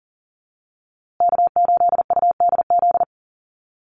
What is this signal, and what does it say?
Morse code, heard as a keyed tone, about 40 wpm, 700 Hz. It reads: K8UDZ
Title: USA Call Signs Repeated 3x 40wpm